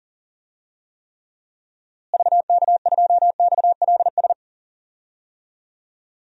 Morse code, heard as a keyed tone, about 40 wpm, 700 Hz. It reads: VK2XLS
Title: DXCC-100 Call Signs - 6 Characters - Repeated 3x - 40wpm